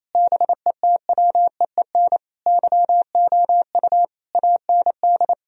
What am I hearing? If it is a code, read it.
Morse code, heard as a keyed tone, about 28 wpm, 690 Hz. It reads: BETWEEN YOU AND
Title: Sentences from Top 1000 Words - Review Part 3 - 28wpm